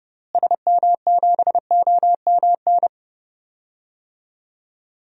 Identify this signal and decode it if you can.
Morse code, heard as a keyed tone, about 30 wpm, 700 Hz. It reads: SM7OMN
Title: DXCC-100 Call Signs - 6 Characters - Repeated 3x - 30wpm